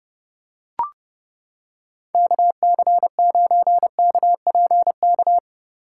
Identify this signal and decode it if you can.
Morse code, heard as a keyed tone, about 29 wpm, 700 Hz. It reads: KC9KPK